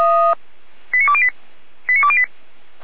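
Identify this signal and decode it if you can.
Five-tone selective call (ZVEI), heard as quick runs of five tones, tones of about 70 ms. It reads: 89208 89208
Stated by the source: ZVEI five-tone call received by SDR, from audio.f85360.feuerwehr-orlamuende.org fire brigade server.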